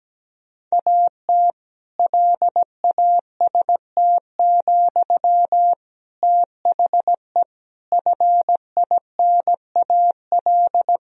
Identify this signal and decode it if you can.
Morse code, heard as a keyed tone, about 17 wpm, 700 Hz. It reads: AT LAST, THE FINAL
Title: Sentences from Top 1000 Words - Review Part 2 - 17wpm